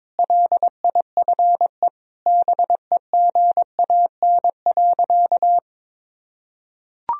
Morse code, heard as a keyed tone, about 21 wpm, 700 Hz. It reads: LIFE BEGAN.
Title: Sentences from Top 1000 Words - Review Part 3 - 22wpm